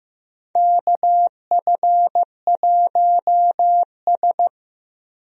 Morse code, heard as a keyed tone, about 15 wpm, 700 Hz. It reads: KF1S
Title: USA Call Signs - 4 Characters - 15wpm